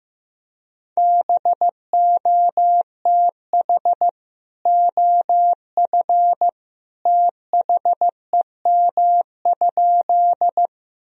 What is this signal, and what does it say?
Morse code, heard as a keyed tone, about 16 wpm, 700 Hz. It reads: BOTH OF THEM?
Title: Sentences from Top 1000 Words - Review Part 1 - 15wpm